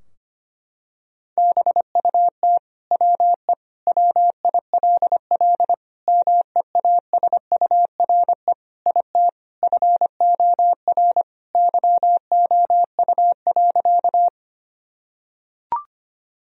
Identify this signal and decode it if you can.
Morse code, heard as a keyed tone, about 25 wpm, 700 Hz. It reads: BUT WE WILL MEASURE IT FOR YOU.
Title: Sentences from Top 1000 Words - Review Part 2 - 25wpm